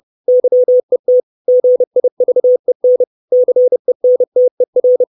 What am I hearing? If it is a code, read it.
Morse code, heard as a keyed tone, about 30 wpm, 500 Hz. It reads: YET GIVEN CENTER
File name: 632 Words 30WPM